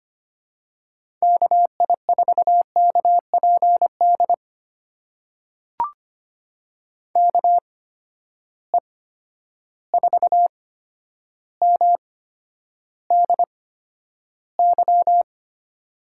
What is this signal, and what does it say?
Morse code, heard as a keyed tone, about 25 wpm, spaced as for 11 wpm, 700 Hz. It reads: KI4KPD KE4MDY
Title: USA Call Signs - 8x Character Spacing - 25wpm